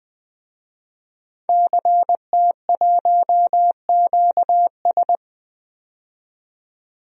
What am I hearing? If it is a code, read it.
Morse code, heard as a keyed tone, about 20 wpm, 700 Hz. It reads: CT1QS
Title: DXCC-100 Call Signs - 5 Characters - 20wpm